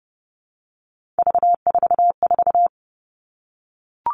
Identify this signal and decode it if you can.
Morse code, heard as a keyed tone, about 30 wpm, 700 Hz. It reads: V44
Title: All Cut Numbers - 3-Digit 30wpm